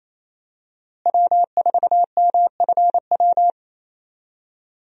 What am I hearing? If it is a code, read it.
Morse code, heard as a keyed tone, about 28 wpm, 700 Hz. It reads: W4MFW
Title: USA Call Signs - 5 Characters - Repeated 2x - 28wpm